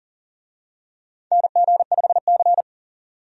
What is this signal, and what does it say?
Morse code, heard as a keyed tone, about 37 wpm, 700 Hz. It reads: NG5C
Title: USA Call Signs - 4 Characters - 40wpm